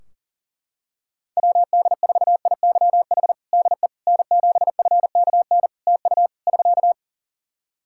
Morse code, heard as a keyed tone, about 40 wpm, 700 Hz. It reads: WD4IYH DE N7FKN TU <SK>
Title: Practice QSO - Exchange 4 - Ending - 40wpm